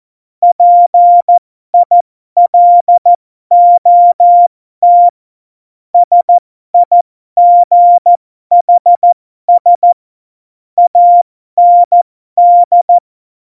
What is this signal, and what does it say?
Morse code, heard as a keyed tone, about 14 wpm, spaced as for 12 wpm, 700 Hz. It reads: PILOT SIGHS AND